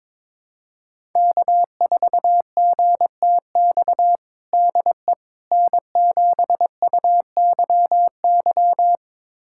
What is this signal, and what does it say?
Morse code, heard as a keyed tone, about 22 wpm, 700 Hz. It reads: K4GTX DE N7UYY
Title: USA Callsign DE Callsign 22wpm